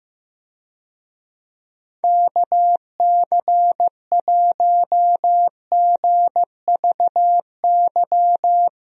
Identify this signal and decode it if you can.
Morse code, heard as a keyed tone, about 15 wpm, 700 Hz. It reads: KC1GVY
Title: USA Call Signs - 6 Characters - Repeated 2x - 15wpm